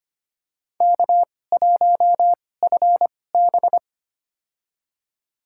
Morse code, heard as a keyed tone, about 25 wpm, spaced as for 18 wpm, 700 Hz. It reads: K1FB
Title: USA Call Signs - 4 Characters - 2x Character Spacing - 25wpm